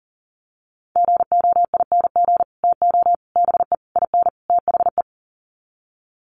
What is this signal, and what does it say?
Morse code, heard as a keyed tone, about 40 wpm, 700 Hz. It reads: GOING TO BE IN THE
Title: Sets of 5 Words Encore 40wpm